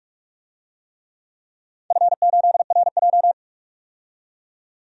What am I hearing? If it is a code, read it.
Morse code, heard as a keyed tone, about 46 wpm, 700 Hz. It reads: F8RJ
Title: DXCC-100 Call Signs - 4 Characters - 45wpm